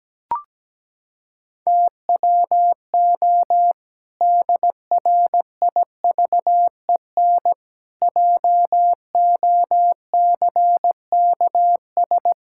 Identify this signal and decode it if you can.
Morse code, heard as a keyed tone, about 17 wpm, 700 Hz. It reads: TWO DRIVEN JOCKS
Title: Pangram Challenge v2 - 17wpm